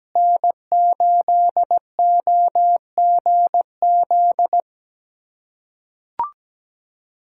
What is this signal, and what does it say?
Morse code, heard as a keyed tone, about 17 wpm, 700 Hz. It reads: N8OGZ
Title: USA Callsign DE Callsign - 5 Characters - 17wpm